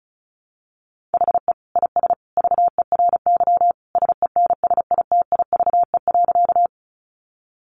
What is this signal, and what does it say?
Morse code, heard as a keyed tone, about 34 wpm, 700 Hz. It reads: HE IS VERY SENSITIVE.
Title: Sentences from Top 1400 Words 35wpm